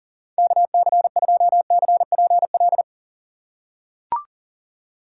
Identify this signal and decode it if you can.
Morse code, heard as a keyed tone, about 40 wpm, 700 Hz. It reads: KC2CPL
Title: USA Call Signs - 6 Characters - 2x Character Spacing - 40wpm